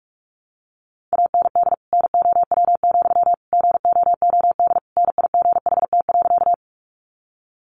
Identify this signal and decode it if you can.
Morse code, heard as a keyed tone, about 44 wpm, 700 Hz. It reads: AND NOW, GOOD NIGHT.
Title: Sentences from Top 300 Words 45wpm